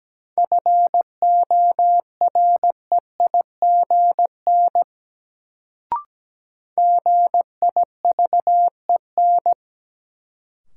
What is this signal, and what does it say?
Morse code, heard as a keyed tone, about 17 wpm, 700 Hz. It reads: FOREIGN GIVEN
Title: Sentences from Top 1400 Words - New Word Drill Alt - 17wpm